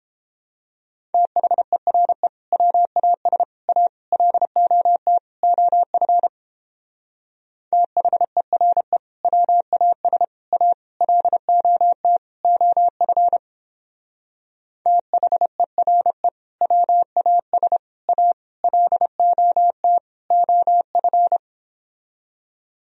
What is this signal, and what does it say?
Morse code, heard as a keyed tone, about 29 wpm, 700 Hz. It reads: THERE WAS A LOT OF THERE WAS A LOT OF THERE WAS A LOT OF